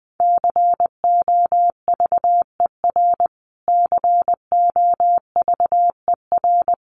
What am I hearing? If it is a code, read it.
Morse code, heard as a keyed tone, about 19 wpm, 700 Hz. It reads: COVER COVER
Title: Top 500 Words Repeated 20wpm